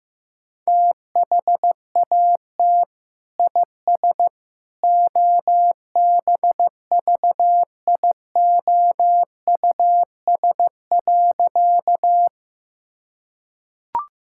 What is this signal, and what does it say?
Morse code, heard as a keyed tone, about 15 wpm, 700 Hz. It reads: THAT IS OBVIOUS.
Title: Sentences from Top 1600 Words 15wpm